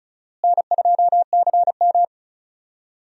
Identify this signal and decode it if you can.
Morse code, heard as a keyed tone, about 36 wpm, 700 Hz. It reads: N2CM
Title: USA Call Signs - 4 Characters - Repeated 3x - 35wpm